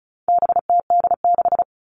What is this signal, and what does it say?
Morse code, heard as a keyed tone, about 34 wpm, 700 Hz. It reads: BTD6
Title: All Cut Numbers - 4-Digit 35wpm